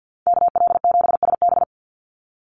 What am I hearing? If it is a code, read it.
Morse code, heard as a keyed tone, about 50 wpm, 700 Hz. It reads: KL7SB
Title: Call Signs - 50wpm